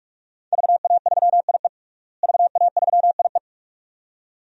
Morse code, heard as a keyed tone, about 46 wpm, 700 Hz. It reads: VA3IE VA3IE
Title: DXCC-100 Call Signs - 5 Characters - Repeated 2x - 45wpm